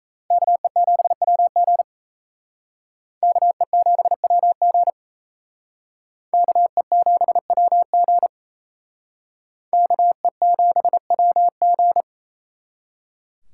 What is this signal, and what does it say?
Morse code, heard as a keyed tone, about 35 wpm, 700 Hz. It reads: KE7WG KE7WG KE7WG KE7WG
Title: Speed Racing - USA Call Signs 28wpm